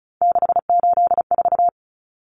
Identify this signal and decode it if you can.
Morse code, heard as a keyed tone, about 34 wpm, 700 Hz. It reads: 684
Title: Numbers - 3-Digit 35wpm